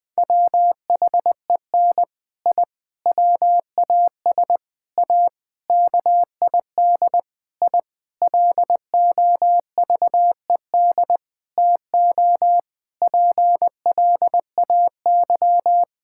Morse code, heard as a keyed tone, about 21 wpm, 700 Hz. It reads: WHEN I WAS A KID I LOVED TO PLAY
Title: Sentences with 5-Word Phrases - Spoken Beginning - 20wpm